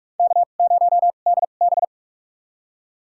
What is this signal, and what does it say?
Morse code, heard as a keyed tone, about 46 wpm, spaced as for 33 wpm, 690 Hz. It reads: K0DB
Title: USA Call Signs - 4 Characters - 2x Character Spacing - 45wpm